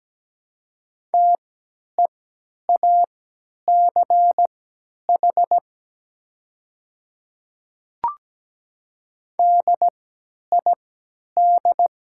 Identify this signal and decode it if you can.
Morse code, heard as a keyed tone, about 17 wpm, spaced as for 10 wpm, 700 Hz. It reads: TEACH DID
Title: 3x Character Spacing - Top 500 Words - 17wpm